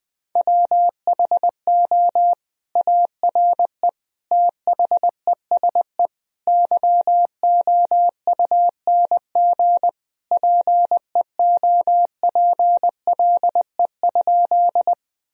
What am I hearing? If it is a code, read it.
Morse code, heard as a keyed tone, about 20 wpm, 700 Hz. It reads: WHO ARE THESE YOUNG PEOPLE?
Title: Sentences from Top 1000 Words - Review Part 1 - 20wpm